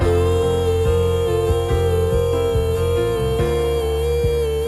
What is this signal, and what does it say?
Busy tone (China, 450 Hz), with noise.